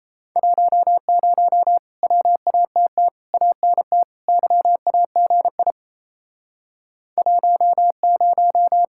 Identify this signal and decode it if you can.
Morse code, heard as a keyed tone, about 37 wpm, 700 Hz. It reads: 10 WATT ANT YAGI 10